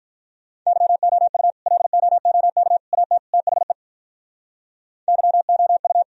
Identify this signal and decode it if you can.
Morse code, heard as a keyed tone, about 44 wpm, 700 Hz. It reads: YOU LOOK AT THE YOU